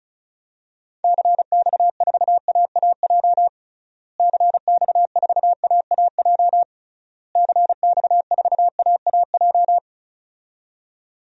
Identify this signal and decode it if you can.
Morse code, heard as a keyed tone, about 36 wpm, 700 Hz. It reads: CX4AAJ CX4AAJ CX4AAJ
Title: DXCC-100 Call Signs - 6 Characters - Repeated 3x - 35wpm